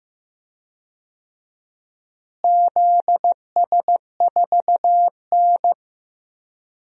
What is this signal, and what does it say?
Morse code, heard as a keyed tone, about 15 wpm, 700 Hz. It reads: ZS4N
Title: DXCC-100 Call Signs - 4 Characters - Repeated 3x - 15wpm